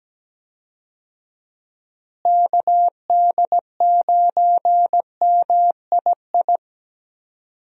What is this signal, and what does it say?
Morse code, heard as a keyed tone, about 17 wpm, 700 Hz. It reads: KD9MII
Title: USA Call Signs - 6 Characters - Repeated 2x - 17wpm